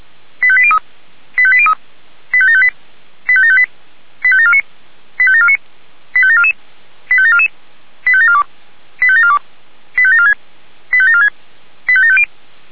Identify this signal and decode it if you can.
Five-tone selective call (ZVEI), heard as quick runs of five tones, tones of about 70 ms. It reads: 85803 85803 85758 85758 85749 85749 85744 85744 85742 85742 85746 85746 85750
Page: SDR/ZVEI